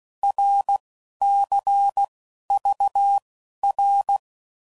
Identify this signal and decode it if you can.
Morse code, heard as a keyed tone, about 16 wpm, spaced as for 12 wpm, 800 Hz. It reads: RCVR
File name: Mixed / Abreviations / Abreviations